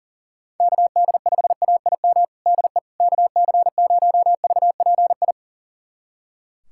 Morse code, heard as a keyed tone, about 40 wpm, 700 Hz. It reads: KD5AIM DE KC0VPI
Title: USA Callsign DE Callsign 40wpm